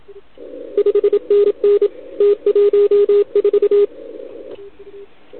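Morse code, heard as a keyed tone, about 28 wpm, 410 Hz, with radio noise. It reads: 5NN T14
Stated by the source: CW from the OKOM DX contest on 80 m, received with a TS480.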